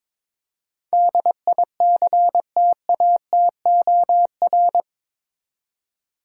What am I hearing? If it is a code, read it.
Morse code, heard as a keyed tone, about 22 wpm, 700 Hz. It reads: DICTATOR
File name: practice / 22wpm / Eight Letter Words - 22wpm